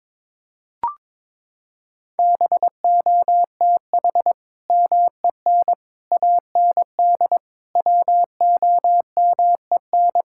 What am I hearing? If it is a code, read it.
Morse code, heard as a keyed tone, about 22 wpm, 700 Hz. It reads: BOTH MEN AND WOMEN
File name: practice / 22wpm / Sentences with 3-Word Phrases - 22wpm